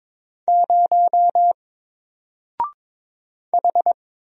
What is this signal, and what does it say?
Morse code, heard as a keyed tone, about 22 wpm, 700 Hz. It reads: 0 H
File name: Single-Letter-Number-22wpm